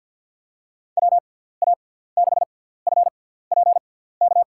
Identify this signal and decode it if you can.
Morse code, heard as a keyed tone, about 53 wpm, spaced as for 17 wpm, 700 Hz. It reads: WA6FPK
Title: USA Call Signs - 6 Characters - 6x Character Spacing - 50wpm